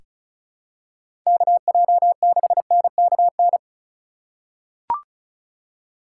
Morse code, heard as a keyed tone, about 36 wpm, 700 Hz. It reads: KJ6NKN
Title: USA Call Signs - 2x Character Spacing - 35wpm